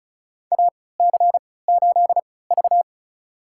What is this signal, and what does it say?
Morse code, heard as a keyed tone, about 35 wpm, spaced as for 20 wpm, 700 Hz. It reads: AC8V
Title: USA Call Signs - 3x Character Spacing - 35wpm